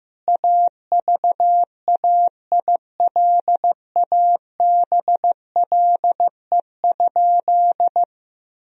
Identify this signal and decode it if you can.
Morse code, heard as a keyed tone, about 15 wpm, 700 Hz. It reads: AVAILABLE?
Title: Sentences from Top 1000 Words - Review Part 3 - 15wpm